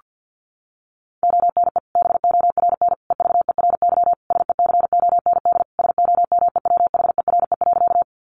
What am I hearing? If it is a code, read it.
Morse code, heard as a keyed tone, about 50 wpm, 700 Hz. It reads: ONE BORN EVERY SECOND SOMEWHERE.
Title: Sentences from Top 1400 Words 50wpm